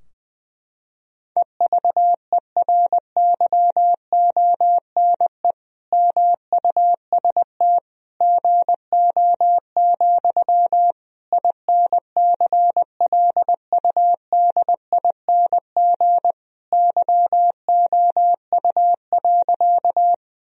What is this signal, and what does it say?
Morse code, heard as a keyed tone, about 20 wpm, 700 Hz. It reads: EVERYONE MUST GO, INCLUDING YOU.
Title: Sentences from Top 1000 Words - Review Part 4 - 20wpm